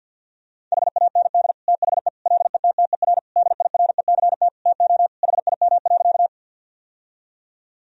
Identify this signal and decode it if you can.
Morse code, heard as a keyed tone, about 52 wpm, 700 Hz. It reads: HAND THE LETTER DIRECT TO HIM.
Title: Sentences from Top 1000 Words - Review Part 2 - 50wpm